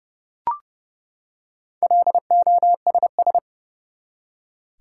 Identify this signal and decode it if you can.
Morse code, heard as a keyed tone, about 32 wpm, 700 Hz. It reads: LOSS
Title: Four Letter Words - 30wpm